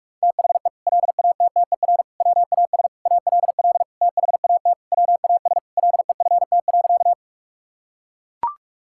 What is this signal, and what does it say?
Morse code, heard as a keyed tone, about 41 wpm, 700 Hz. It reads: THE LATTER WAS ALL THAT WAS LEFT.
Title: Sentences from Top 1600 Words 45wpm